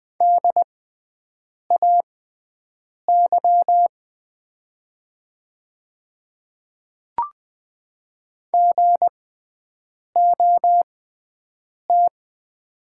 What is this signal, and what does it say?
Morse code, heard as a keyed tone, about 20 wpm, spaced as for 7 wpm, 700 Hz. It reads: DAY GOT
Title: 6x Character Spacing - Top 500 Words - 20wpm